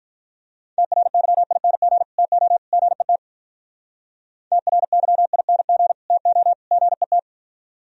Morse code, heard as a keyed tone, about 43 wpm, 700 Hz. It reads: TRYING TO GET TRYING TO GET
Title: Speed-Racing - Sets of 3 Words 35wpm